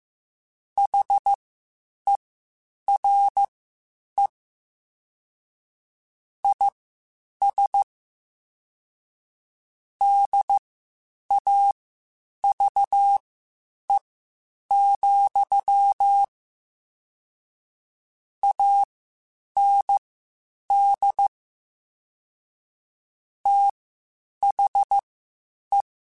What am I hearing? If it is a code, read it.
Morse code, heard as a keyed tone, about 15 wpm, spaced as for 8 wpm, 800 Hz. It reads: HERE IS DAVE, AND THE